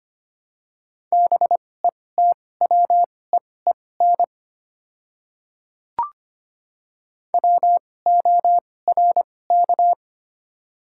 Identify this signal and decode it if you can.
Morse code, heard as a keyed tone, about 25 wpm, spaced as for 18 wpm, 700 Hz. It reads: BETWEEN WORK
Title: 2x Character Spacing - Top 300 Words - 25wpm